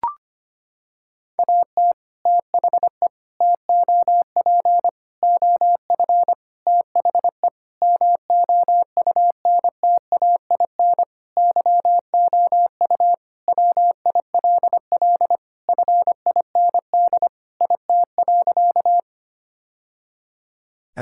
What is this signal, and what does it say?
Morse code, heard as a keyed tone, about 25 wpm, 700 Hz. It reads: AT THE TOP OF THE MOUNTAIN YOU WILL FIND IT.
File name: practice / 25wpm / Sentences with 3-Word Phrases - 25wpm